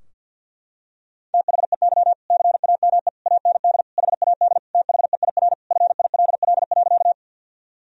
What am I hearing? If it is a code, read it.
Morse code, heard as a keyed tone, about 52 wpm, 700 Hz. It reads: THEY CAME AND HAD THEIR FILL.
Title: Sentences from Top 1000 Words - Review Part 3 - 50wpm